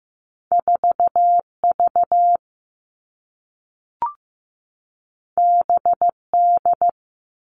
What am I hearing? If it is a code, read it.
Morse code, heard as a keyed tone, about 15 wpm, 700 Hz. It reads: 4V BD
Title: All Cut Numbers - 2-Digit 15wpm